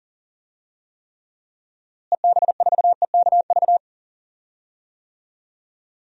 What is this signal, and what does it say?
Morse code, heard as a keyed tone, about 40 wpm, 700 Hz. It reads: EB4EKV
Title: DXCC-100 Call Signs - 6 Characters - Repeated 3x - 40wpm